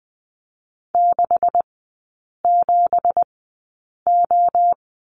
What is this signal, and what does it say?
Morse code, heard as a keyed tone, about 20 wpm, spaced as for 8 wpm, 700 Hz. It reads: 67O
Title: 166 Lesson 27e - Number 7 - Three Characters 20wpm